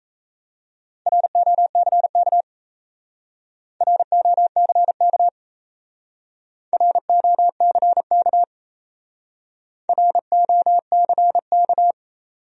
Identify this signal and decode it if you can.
Morse code, heard as a keyed tone, about 35 wpm, 700 Hz. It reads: ROCK ROCK ROCK ROCK
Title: Speed-Racing Top 500 Words 28wpm